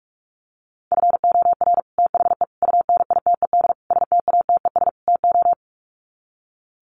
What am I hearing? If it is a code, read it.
Morse code, heard as a keyed tone, about 44 wpm, 700 Hz. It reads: FOR THE UNITED STATES TO
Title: Sets of 5 Words Encore 45wpm